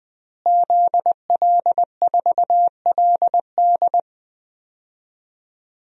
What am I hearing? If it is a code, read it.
Morse code, heard as a keyed tone, about 20 wpm, 700 Hz. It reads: ZL4LD
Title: DXCC-100 Call Signs - 5 Characters - Repeated 3x - 20wpm